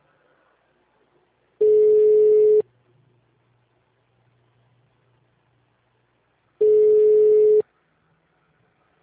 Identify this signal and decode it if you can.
Ringback tone (Europe, 425 Hz).
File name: out- -2055- - - .10311